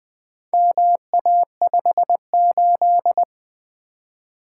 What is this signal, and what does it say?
Morse code, heard as a keyed tone, about 20 wpm, 700 Hz. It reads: MA58
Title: Four Letter-Number Combinations - 20wpm